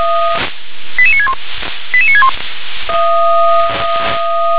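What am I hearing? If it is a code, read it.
Five-tone selective call (ZVEI), heard as quick runs of five tones, tones of about 70 ms. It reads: 88961 88961
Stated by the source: Five-tone ZVEI selective call 88960 received by SDR.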